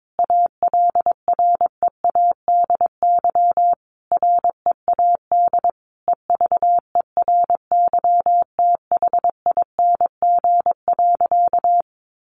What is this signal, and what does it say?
Morse code, heard as a keyed tone, about 22 wpm, 700 Hz. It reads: ALREADY READ EVERYTHING.
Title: Sentences from Top 700 Words 22wpm